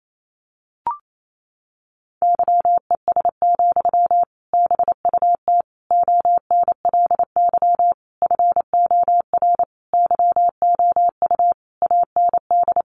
Morse code, heard as a keyed tone, about 28 wpm, 700 Hz. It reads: YES, BUT ONLY FOR YOU AND
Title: Sentences from Top 500 Words 28wpm